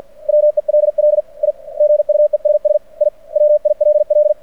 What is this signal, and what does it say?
Morse code, heard as a keyed tone, about 49 wpm, 600 Hz. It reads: WELL I GUESS I WILL
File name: archived / CRSnet